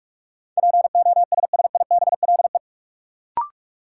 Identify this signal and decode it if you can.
Morse code, heard as a keyed tone, about 44 wpm, 700 Hz. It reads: POSSIBLE
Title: Sentences from Top 500 Words - New Word Drill - 45wpm